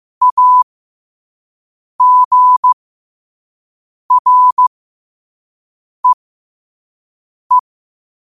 Morse code, heard as a keyed tone, about 15 wpm, spaced as for 5 wpm, 1000 Hz. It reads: AGREE
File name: qotdfiles / 5wpm